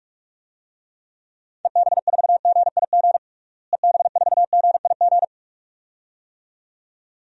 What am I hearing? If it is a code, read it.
Morse code, heard as a keyed tone, about 46 wpm, 700 Hz. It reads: EB4GIG EB4GIG
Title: DXCC-100 Call Signs - 6 Characters - Repeated 2x - 45wpm